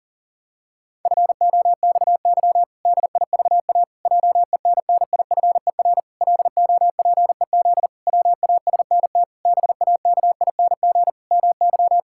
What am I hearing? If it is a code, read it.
Morse code, heard as a keyed tone, about 40 wpm, 700 Hz. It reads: FOXY DIVA JENNIFER LOPEZ WASNT BAKING MY
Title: Pangram Challenge v2 - 40wpm